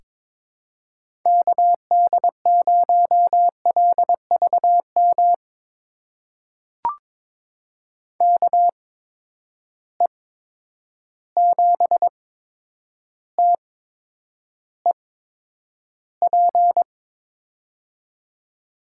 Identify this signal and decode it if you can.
Morse code, heard as a keyed tone, about 22 wpm, spaced as for 9 wpm, 700 Hz. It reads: KD0LVM KE7TEP
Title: USA Call Signs - 6 Characters - 8x Character Spacing - 22wpm